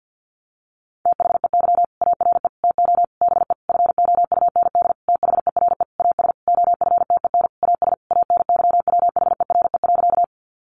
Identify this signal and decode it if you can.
Morse code, heard as a keyed tone, about 50 wpm, 700 Hz. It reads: THEY ARE TO BE FOUND THERE AS OFTEN AS ANYWHERE.
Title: Sentences from Top 1400 Words 50wpm